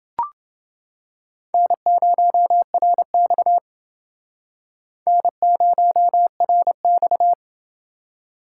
Morse code, heard as a keyed tone, about 29 wpm, 700 Hz. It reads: N0RX N0RX